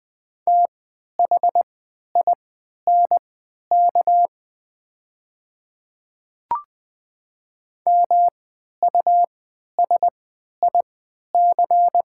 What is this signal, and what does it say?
Morse code, heard as a keyed tone, about 21 wpm, spaced as for 11 wpm, 700 Hz. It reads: THINK MUSIC
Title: 3x Character Spacing - Top 300 Words - 20wpm